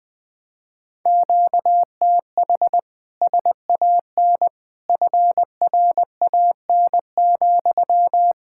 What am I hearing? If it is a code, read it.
Morse code, heard as a keyed tone, about 20 wpm, 700 Hz. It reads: QTH SAN FRAN,
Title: Speed-Racing Common QSO Phrases Encore - 17wpm